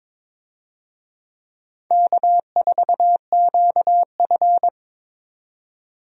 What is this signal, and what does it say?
Morse code, heard as a keyed tone, about 22 wpm, 700 Hz. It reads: K4QF
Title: USA Call Signs - 4 Characters - 22wpm